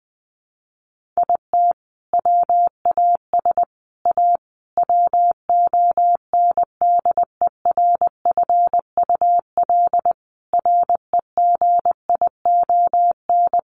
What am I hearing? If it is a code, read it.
Morse code, heard as a keyed tone, about 20 wpm, 700 Hz. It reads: IT WAS A WONDERFUL REGION
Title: Sentences from Top 1500 Words 20wpm